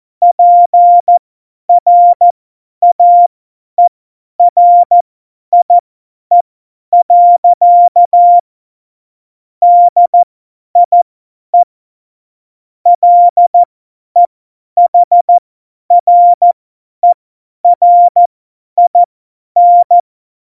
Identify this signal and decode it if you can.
Morse code, heard as a keyed tone, about 14 wpm, spaced as for 10 wpm, 700 Hz. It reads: PRAERIE. DIE LEHRERIN
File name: Morsekurs / Witze / Witz 21 Juli 2020, 50 F70